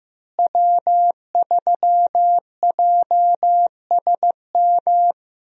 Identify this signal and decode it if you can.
Morse code, heard as a keyed tone, about 15 wpm, 700 Hz. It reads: W3JSM
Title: USA Call Signs - 5 Characters - Repeated 3x - 15wpm